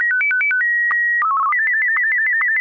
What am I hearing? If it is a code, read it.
SSTV (slow-scan television), Martin 1.